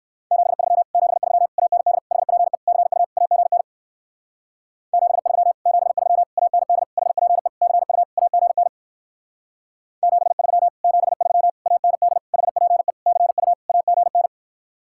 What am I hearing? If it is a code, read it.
Morse code, heard as a keyed tone, about 58 wpm, 700 Hz. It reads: 73 73 AND HPE CU AGN 73 73 AND HPE CU AGN 73 73 AND HPE CU AGN